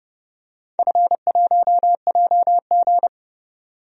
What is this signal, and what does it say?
Morse code, heard as a keyed tone, about 30 wpm, 700 Hz. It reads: F1JG
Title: DXCC-100 Call Signs - 4 Characters - 30wpm